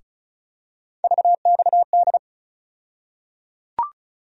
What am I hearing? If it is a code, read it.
Morse code, heard as a keyed tone, about 36 wpm, 700 Hz. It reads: VXD
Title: USA Call Sign Postfixes - 3 Letters 35wpm